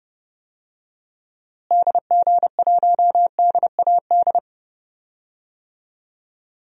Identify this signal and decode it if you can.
Morse code, heard as a keyed tone, about 30 wpm, 700 Hz. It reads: DG1DAD
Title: DXCC-100 Call Signs - 6 Characters - Repeated 2x - 30wpm